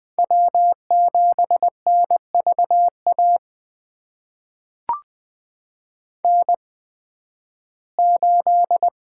Morse code, heard as a keyed tone, about 20 wpm, spaced as for 13 wpm, 700 Hz. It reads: W7NVA N8
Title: USA Call Signs - 5 Characters - 8x Character Spacing - 20wpm